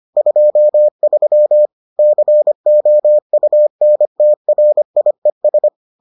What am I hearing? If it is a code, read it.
Morse code, heard as a keyed tone, about 25 wpm, 600 Hz. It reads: 23 COUNTRIES